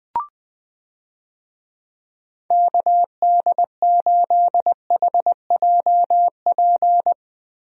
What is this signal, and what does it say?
Morse code, heard as a keyed tone, about 20 wpm, 700 Hz. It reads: KD8HJP
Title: USA Call Signs - 6 Characters - Repeated 2x - 20wpm